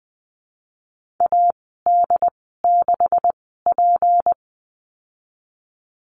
Morse code, heard as a keyed tone, about 20 wpm, spaced as for 15 wpm, 700 Hz. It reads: AD6P
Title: 161 Lesson 26g - Letter g - Call Signs 20wpm